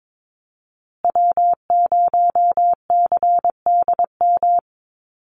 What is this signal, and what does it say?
Morse code, heard as a keyed tone, about 22 wpm, 700 Hz. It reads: W0CDM
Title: USA Call Signs Repeated 3x 22wpm